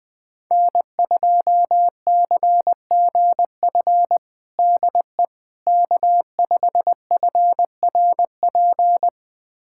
Morse code, heard as a keyed tone, about 20 wpm, 700 Hz. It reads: N2CGF DE K5FRP
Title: USA Callsign DE Callsign - 5 Characters - 20wpm